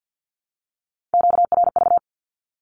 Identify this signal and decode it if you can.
Morse code, heard as a keyed tone, about 48 wpm, 700 Hz. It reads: QRV
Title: Q-Codes 50wpm